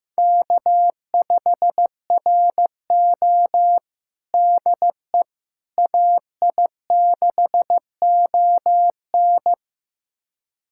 Morse code, heard as a keyed tone, about 15 wpm, 700 Hz. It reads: K5RO DE AI6ON
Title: USA Callsign DE Callsign 15wpm